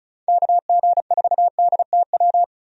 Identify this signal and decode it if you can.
Morse code, heard as a keyed tone, about 34 wpm, 700 Hz. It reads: KG4DTW
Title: USA Call Signs - 6 Characters - 35wpm